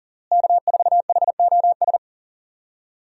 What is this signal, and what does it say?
Morse code, heard as a keyed tone, about 40 wpm, 700 Hz. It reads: K4HOS
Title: USA Call Signs - 5 Characters - Repeated 3x - 40wpm